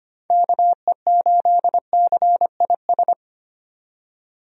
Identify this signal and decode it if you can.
Morse code, heard as a keyed tone, about 25 wpm, 700 Hz. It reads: KE8CIS
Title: USA Call Signs - 6 Characters - 25wpm